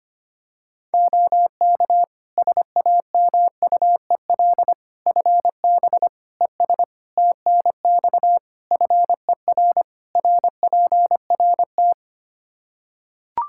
Morse code, heard as a keyed tone, about 25 wpm, 700 Hz. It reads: OK SAMUEL FB ES TNX FER RPRT
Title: Practice QSO - Exchange 2 - Info - 25wpm